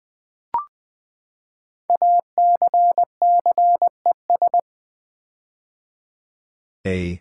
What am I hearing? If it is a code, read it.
Morse code, heard as a keyed tone, about 20 wpm, 700 Hz. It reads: ACCES
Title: Build A Word - 5 Letters At A Time - 10-Letter Words - 20wpm